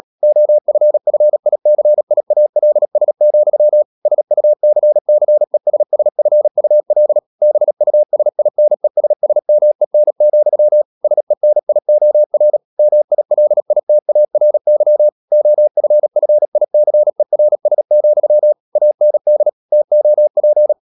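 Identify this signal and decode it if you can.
Morse code, heard as a keyed tone, about 37 wpm, 600 Hz. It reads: OFFICIALS, SUCCESSFUL BUSINESSMEN, SENIOR MILITARY OFFICERS, AND TOP
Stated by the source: Morse code practice at 35–39 wpm sending English text of Ken Follett's novel Never, from CW med Gnister.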